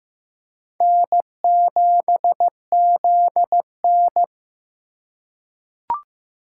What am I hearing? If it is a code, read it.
Morse code, heard as a keyed tone, about 15 wpm, 700 Hz. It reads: N7ZN
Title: K1USN SST QSO Practice - 15wpm